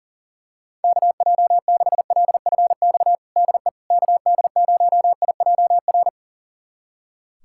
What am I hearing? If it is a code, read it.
Morse code, heard as a keyed tone, about 40 wpm, 700 Hz. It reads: KJ6LFX DE KD0IJR
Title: USA Callsign DE Callsign - 6 Characters - 40wpm